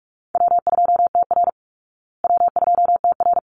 Morse code, heard as a keyed tone, about 44 wpm, 700 Hz. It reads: W2TR W2TR
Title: Easy Call Signs Repeated 2x - 45wpm